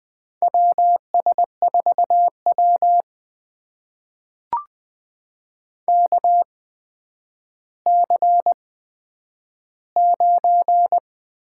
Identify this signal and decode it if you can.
Morse code, heard as a keyed tone, about 20 wpm, spaced as for 10 wpm, 700 Hz. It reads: WS4W KC9
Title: USA Call Signs - 8x Character Spacing - 20wpm